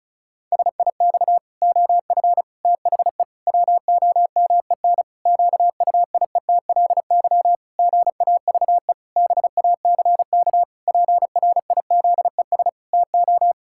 Morse code, heard as a keyed tone, about 34 wpm, 700 Hz. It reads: SIX OF THE WOMEN QUIETLY GAVE BACK PRIZES TO
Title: Pangram Challenge v2 - 35wpm